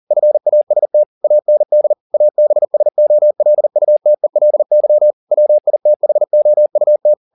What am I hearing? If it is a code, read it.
Morse code, heard as a keyed tone, about 40 wpm, 600 Hz. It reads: FAST AND ABSOLUTELY WITHOUT
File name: the old man and the sea 2 40wpm